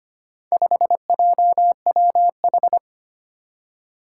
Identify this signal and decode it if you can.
Morse code, heard as a keyed tone, about 25 wpm, 700 Hz. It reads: 5JWH
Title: Four Challenging Letter-Number Combinations 25wpm